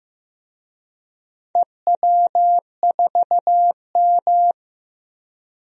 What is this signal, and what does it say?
Morse code, heard as a keyed tone, about 15 wpm, 700 Hz. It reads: EW4M
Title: DXCC-100 Call Signs - 4 Characters - 15wpm